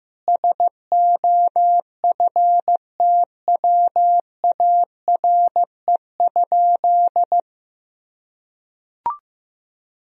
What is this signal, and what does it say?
Morse code, heard as a keyed tone, about 15 wpm, 700 Hz. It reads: SOFTWARE?
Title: Sentences from Top 1000 Words - Review Part 3 - 15wpm